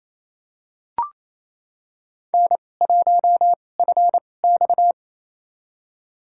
Morse code, heard as a keyed tone, about 28 wpm, spaced as for 20 wpm, 700 Hz. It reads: N1FX
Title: USA Call Signs - 4 Characters - 2x Character Spacing - 28wpm